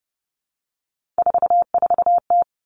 Morse code, heard as a keyed tone, about 30 wpm, 700 Hz. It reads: 44T
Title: Three Challenging Letter-Number Combinations 30wpm